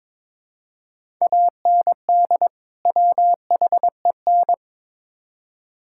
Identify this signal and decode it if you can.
Morse code, heard as a keyed tone, about 22 wpm, 700 Hz. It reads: AND WHEN
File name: practice / 22wpm / Build A Phrase - 5-Word Phrases - 22wpm